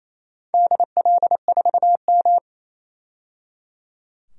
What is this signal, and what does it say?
Morse code, heard as a keyed tone, about 29 wpm, 690 Hz. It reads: DL4M
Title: DXCC-100 Call Signs - 4 Characters - 28wpm